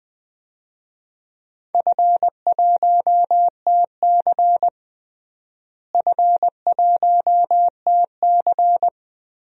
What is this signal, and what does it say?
Morse code, heard as a keyed tone, about 20 wpm, 700 Hz. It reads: F1TC F1TC
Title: DXCC-100 Call Signs - 4 Characters - Repeated 2x - 20wpm